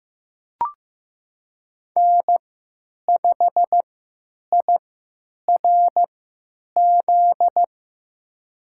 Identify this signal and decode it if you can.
Morse code, heard as a keyed tone, about 15 wpm, spaced as for 9 wpm, 700 Hz. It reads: N5IRZ
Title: USA Call Signs - 5 Characters - 3x Character Spacing - 15wpm